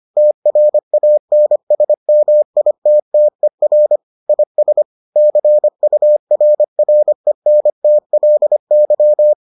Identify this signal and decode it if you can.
Morse code, heard as a keyed tone, about 25 wpm, 600 Hz. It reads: TRANSMITTER IS CURRENTLY